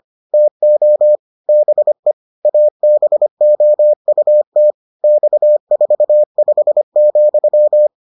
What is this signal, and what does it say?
Morse code, heard as a keyed tone, about 25 wpm, 600 Hz. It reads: TO BE ABOUT X45,